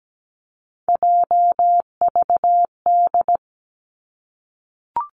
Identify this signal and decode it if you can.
Morse code, heard as a keyed tone, about 17 wpm, 700 Hz. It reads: JVD
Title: Three Challenging Letter-Number Combinations 17wpm